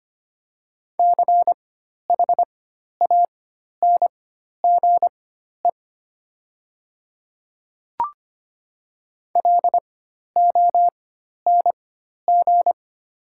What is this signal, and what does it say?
Morse code, heard as a keyed tone, about 25 wpm, spaced as for 12 wpm, 700 Hz. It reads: CHANGE LONG
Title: 4x Character Spacing - Top 200 Words - 25wpm